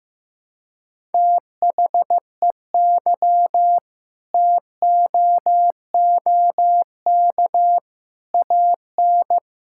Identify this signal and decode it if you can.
Morse code, heard as a keyed tone, about 15 wpm, 700 Hz. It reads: THEY TOOK AN
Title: Sentences from Top 1000 Words - Review Part 6 - 15wpm